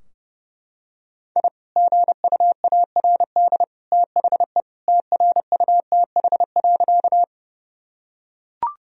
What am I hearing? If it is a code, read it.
Morse code, heard as a keyed tone, about 30 wpm, 700 Hz. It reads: I GUARD THE TRUTH.
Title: Sentences from Top 1000 Words - Review Part 4 - 30wpm